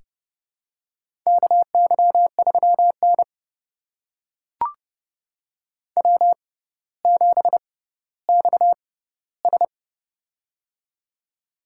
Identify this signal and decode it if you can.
Morse code, heard as a keyed tone, about 30 wpm, spaced as for 15 wpm, 700 Hz. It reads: KY3N W7XS
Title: USA Call Signs - 4 Characters - 6x Character Spacing - 30wpm